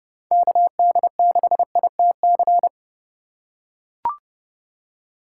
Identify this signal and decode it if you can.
Morse code, heard as a keyed tone, about 28 wpm, 700 Hz. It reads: KD6ITC